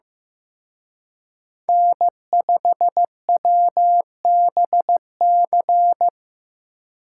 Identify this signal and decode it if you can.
Morse code, heard as a keyed tone, about 18 wpm, 700 Hz. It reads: N5WBC
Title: Speed Racing - USA 1x3 Call Signs 15wpm